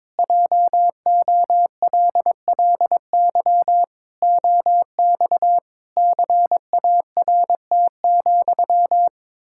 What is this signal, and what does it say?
Morse code, heard as a keyed tone, about 22 wpm, 700 Hz. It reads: JOLLY OX CART,
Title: Pangram Challenge v2 - 22wpm